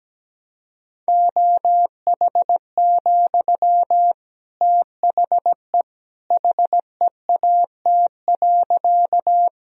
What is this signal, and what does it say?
Morse code, heard as a keyed tone, about 17 wpm, 700 Hz. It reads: OH, THE HEAT.
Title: Sentences from Top 1000 Words - Review Part 3 - 17wpm